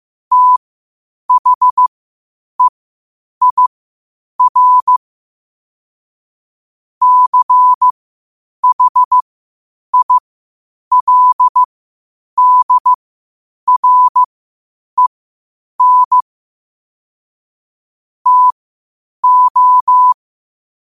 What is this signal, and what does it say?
Morse code, heard as a keyed tone, about 15 wpm, spaced as for 8 wpm, 1000 Hz. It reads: THEIR CHILDREN TO
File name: qotdfiles / 8wpm